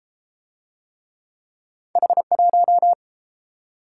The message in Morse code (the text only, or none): H1